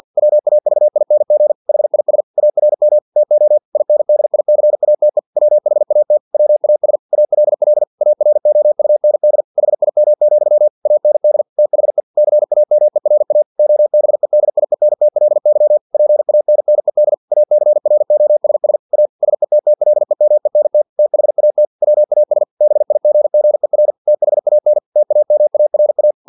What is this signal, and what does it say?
Morse code, heard as a keyed tone, about 50 wpm, 600 Hz. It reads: WAVING HIS ARM TO INDICATE WHAT WAS ALL AROUND HIM, AND THE CAMERA OBEDIENTLY PANNED ACROSS A SETTLEMENT THAT WAS BIGGER THAN TAMARA
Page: Never 49wpm